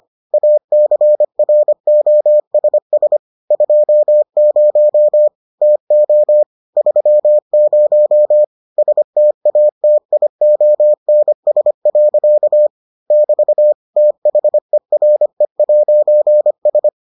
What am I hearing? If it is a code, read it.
Morse code, heard as a keyed tone, about 25 wpm, 600 Hz. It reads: ACROSS 20 TO 30 STATIONS. <BT> THERE'S